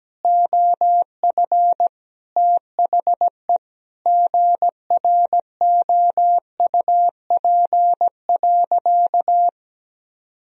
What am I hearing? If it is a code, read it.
Morse code, heard as a keyed tone, about 17 wpm, 700 Hz. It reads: OF THE GROUP.
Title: Sentences from Top 1000 Words - Review Part 1 - 17wpm